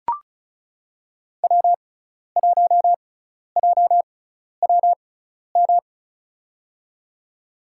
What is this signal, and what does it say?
Morse code, heard as a keyed tone, about 36 wpm, spaced as for 12 wpm, 700 Hz. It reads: W1JWM
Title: USA Call Signs - 5 Characters - 6x Character Spacing - 35wpm